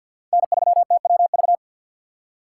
Morse code, heard as a keyed tone, about 47 wpm, 700 Hz. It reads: N3TWV
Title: USA Call Signs - 5 Characters - 2x Character Spacing - 50wpm